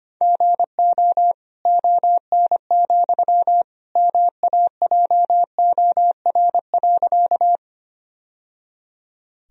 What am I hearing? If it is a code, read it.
Morse code, heard as a keyed tone, about 25 wpm, 700 Hz. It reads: GO ON, MAJOR.
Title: Sentences from Top 1000 Words - Review Part 4 - 25wpm